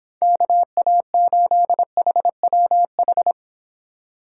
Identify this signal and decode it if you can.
Morse code, heard as a keyed tone, about 29 wpm, 700 Hz. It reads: KA8HWH